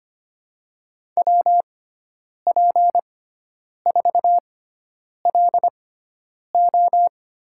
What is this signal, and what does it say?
Morse code, heard as a keyed tone, about 25 wpm, spaced as for 9 wpm, 700 Hz. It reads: WP4LO
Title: USA Call Signs - 6x Character Spacing - 25wpm